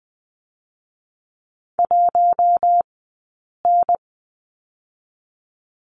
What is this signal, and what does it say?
Morse code, heard as a keyed tone, about 20 wpm, 700 Hz. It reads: 1 N
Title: 052 Lesson 10d - Letter r - Two Characters 20wpm